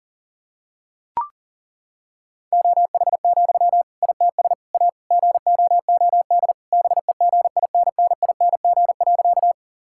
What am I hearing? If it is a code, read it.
Morse code, heard as a keyed tone, about 40 wpm, 700 Hz. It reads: OH, ITS A GOOD BEGINNING.
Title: Sentences from Top 1000 Words - Review Part 5 - 40wpm